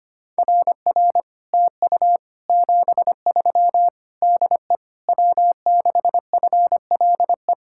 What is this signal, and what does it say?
Morse code, heard as a keyed tone, about 30 wpm, 700 Hz. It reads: RR TU 73 DE W6FLE
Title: Speed-Racing Common QSO Phrases - 25wpm